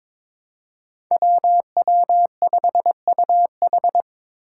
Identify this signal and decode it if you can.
Morse code, heard as a keyed tone, about 22 wpm, 700 Hz. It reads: WW5UH
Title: Five Challenging Letter-Number Combinations 22wpm